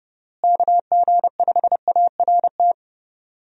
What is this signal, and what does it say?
Morse code, heard as a keyed tone, about 30 wpm, 700 Hz. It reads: KG5ART
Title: USA Call Signs - 6 Characters - Repeated 2x - 30wpm